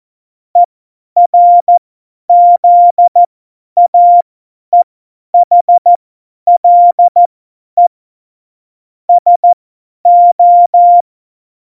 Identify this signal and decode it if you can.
Morse code, heard as a keyed tone, about 14 wpm, spaced as for 10 wpm, 700 Hz. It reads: ERZAEHLE SO
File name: Morsekurs / Witze / Witz 20. Oktober 2020, 50 F70